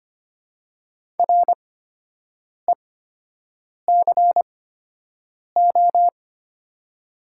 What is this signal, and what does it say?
Morse code, heard as a keyed tone, about 25 wpm, spaced as for 7 wpm, 700 Hz. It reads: RECO
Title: 8x Character Spacing - Top 500 Words - 25wpm